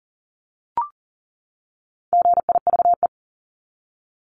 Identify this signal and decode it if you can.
Morse code, heard as a keyed tone, about 40 wpm, 700 Hz. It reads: GIVE
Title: Top 500 Words 40wpm